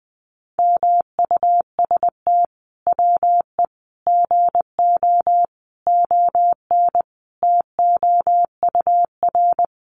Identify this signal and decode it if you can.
Morse code, heard as a keyed tone, about 20 wpm, 700 Hz. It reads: MUST WE GO ON TOUR
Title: Sentences from Top 1400 Words 20wpm